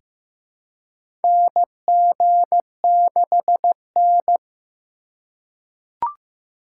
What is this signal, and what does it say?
Morse code, heard as a keyed tone, about 15 wpm, 700 Hz. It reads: NG6N
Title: USA Call Signs - 4 Characters - 2x Character Spacing - 15wpm